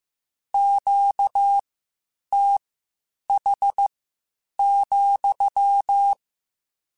Morse code, heard as a keyed tone, about 15 wpm, spaced as for 8 wpm, 800 Hz. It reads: QTH,
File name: Q Calls / Q Calls-mp3s / Q CALLS SPSL @13